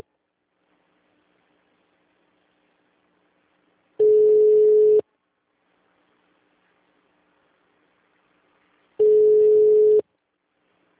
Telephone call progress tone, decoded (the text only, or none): ringback tone